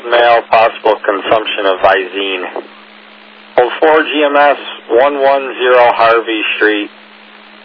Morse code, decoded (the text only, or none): none